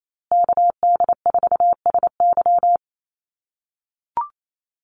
Morse code, heard as a keyed tone, about 28 wpm, 700 Hz. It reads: KD4SY
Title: USA Call Signs Repeated 3x 28wpm